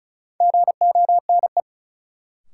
Morse code, heard as a keyed tone, about 34 wpm, 700 Hz. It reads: GONE
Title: Sentences from Top 1200 Words - New Word Drill Alt - 35wpm